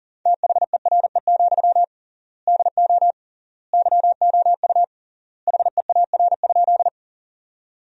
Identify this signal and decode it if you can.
Morse code, heard as a keyed tone, about 40 wpm, 700 Hz. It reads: THERE, DO YOU HEAR?
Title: 3x Word-Spacing - Sentences from Top 300 Words - 40wpm